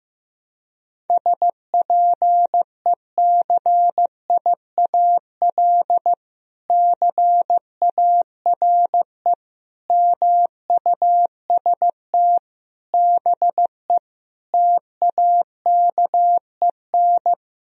Morse code, heard as a keyed tone, about 15 wpm, 700 Hz. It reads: SPECIAL CARE MUST BE TAKEN
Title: Sentences from Top 1000 Words - Review Part 4 - 15wpm